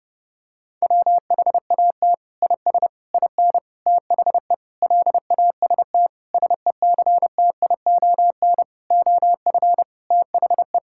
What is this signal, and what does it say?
Morse code, heard as a keyed tone, about 30 wpm, 700 Hz. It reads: WHAT IS IN THE LAST SECTION OF THE
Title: Sentences from Top 1000 Words - Review Part 5 - 30wpm